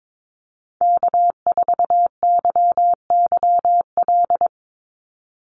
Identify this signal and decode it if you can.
Morse code, heard as a keyed tone, about 22 wpm, 700 Hz. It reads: K4YYL
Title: Call Signs Repeated 2x - 22wpm